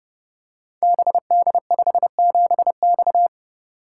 Morse code, heard as a keyed tone, about 28 wpm, 700 Hz. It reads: BD57X
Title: Code Groups - 5 Letter:Numbers - 30wpm